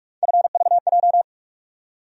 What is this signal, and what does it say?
Morse code, heard as a keyed tone, about 47 wpm, 700 Hz. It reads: FVJ
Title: USA Call Sign Postfixes - 3 Letters 45wpm